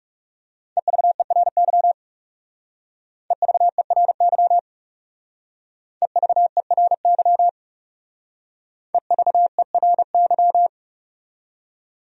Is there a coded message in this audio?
Yes: Morse code from 0.8 to 10.7 s, heard as a keyed tone, about 40 wpm, 700 Hz. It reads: EVERY EVERY EVERY EVERY